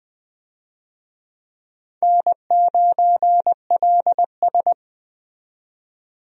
Morse code, heard as a keyed tone, about 20 wpm, 700 Hz. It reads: N9LS
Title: USA Call Signs - 4 Characters - 20wpm